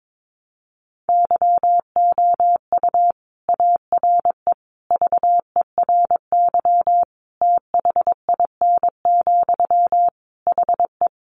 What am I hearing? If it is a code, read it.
Morse code, heard as a keyed tone, about 22 wpm, 700 Hz. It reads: YOU ARE VERY THIN, HE
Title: Sentences from Top 1200 Words 22wpm